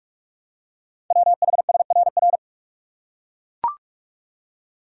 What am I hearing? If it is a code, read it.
Morse code, heard as a keyed tone, about 44 wpm, 690 Hz. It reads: WHSRR
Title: Code Groups - 5 Letters - 45wpm